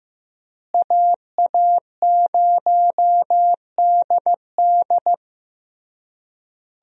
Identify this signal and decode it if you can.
Morse code, heard as a keyed tone, about 16 wpm, 700 Hz. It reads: AA0DD
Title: USA Call Signs - 5 Characters - Repeated 3x - 15wpm